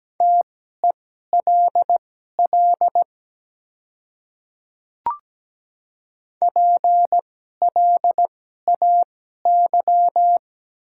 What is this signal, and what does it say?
Morse code, heard as a keyed tone, about 17 wpm, spaced as for 12 wpm, 700 Hz. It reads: TELL PLAY
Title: 2x Character Spacing - Top 200 Words - 17wpm